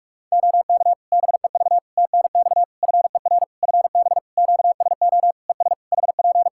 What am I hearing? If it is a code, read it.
Morse code, heard as a keyed tone, about 46 wpm, 700 Hz. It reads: OK BEV TNX FER FB QSO ES HP
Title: Practice QSO - Exchange 4 - Ending - 45wpm